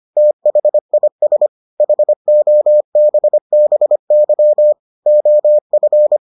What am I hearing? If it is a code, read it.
Morse code, heard as a keyed tone, about 25 wpm, 600 Hz. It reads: THIS HOBBY OF